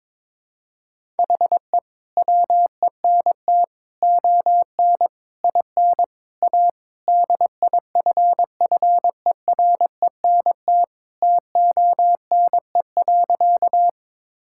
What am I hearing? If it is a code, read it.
Morse code, heard as a keyed tone, about 22 wpm, 700 Hz. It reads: HE WENT ON IN A DIFFERENT TONE.
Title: Sentences from Top 1600 Words 22wpm